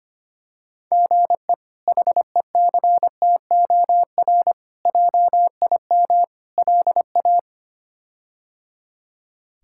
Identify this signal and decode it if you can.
Morse code, heard as a keyed tone, about 25 wpm, 700 Hz. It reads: GE HECTOR JIM LA
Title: K1USN SST QSO Practice - 25wpm